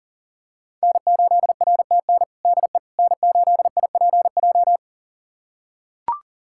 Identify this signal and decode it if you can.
Morse code, heard as a keyed tone, about 40 wpm, 700 Hz. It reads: N8RTN DE N8IPJ
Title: USA Callsign DE Callsign - 5 Characters - 40wpm